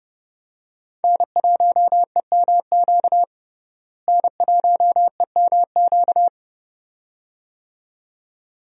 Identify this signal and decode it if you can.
Morse code, heard as a keyed tone, about 30 wpm, 700 Hz. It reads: N1EMQ N1EMQ
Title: USA Call Signs - 5 Characters - Repeated 2x - 30wpm